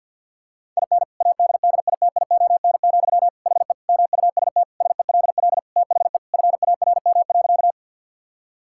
Morse code, heard as a keyed tone, about 50 wpm, 700 Hz. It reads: IN ADDITION, HE MUST SELL THE FARM.
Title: Sentences from Top 1000 Words - Review Part 6 - 50wpm